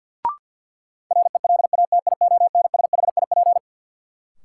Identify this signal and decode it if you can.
Morse code, heard as a keyed tone, about 53 wpm, 700 Hz. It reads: RELATIONSHIP
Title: Sentences from Top 1100 Words - New Word Drill Alt - 50wpm